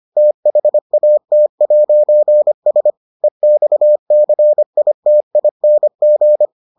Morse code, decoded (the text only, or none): THAT'S EXCITING